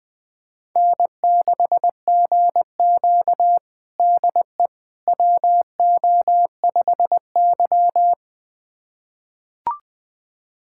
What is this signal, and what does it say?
Morse code, heard as a keyed tone, about 20 wpm, 700 Hz. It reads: N6GQ DE WO5Y
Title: USA Callsign DE Callsign - 4 Characters - 20wpm